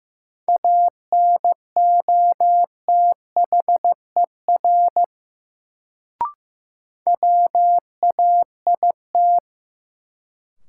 Morse code, heard as a keyed tone, about 15 wpm, 700 Hz. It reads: ANOTHER WAIT
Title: Sets of 5 Words Encore - New Word Drill - 15wpm